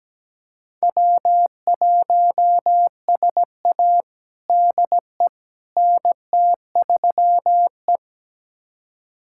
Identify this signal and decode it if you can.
Morse code, heard as a keyed tone, about 17 wpm, 700 Hz. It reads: W1SA DE NT3E
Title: USA Callsign DE Callsign - 4 Characters - 17wpm